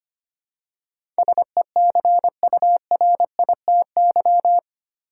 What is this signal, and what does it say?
Morse code, heard as a keyed tone, about 25 wpm, 700 Hz. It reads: SECURITY
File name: practice / 25wpm / 8 Letter Words - First 3 Letters Given - 25wpm